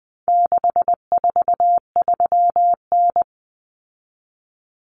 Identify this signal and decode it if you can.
Morse code, heard as a keyed tone, about 20 wpm, 700 Hz. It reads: 643N
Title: Common Cut Numbers - 4-Digit 20wpm